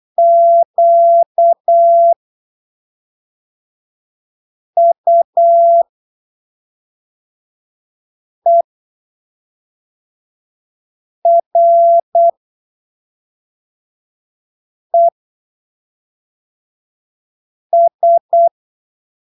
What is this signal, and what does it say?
Morse code, heard as a keyed tone, about 8 wpm, spaced as for 3 wpm, 680 Hz. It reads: QUERES
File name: arquivos / texto / texto 26 08ppm